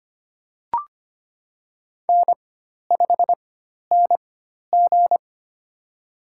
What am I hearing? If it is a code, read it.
Morse code, heard as a keyed tone, about 25 wpm, spaced as for 12 wpm, 700 Hz. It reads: N5NG
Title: USA Call Signs - 4x Character Spacing - 25wpm